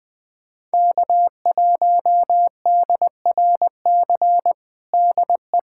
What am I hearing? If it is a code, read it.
Morse code, heard as a keyed tone, about 20 wpm, 700 Hz. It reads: K1DRC DE
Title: USA Callsign DE Callsign 20wpm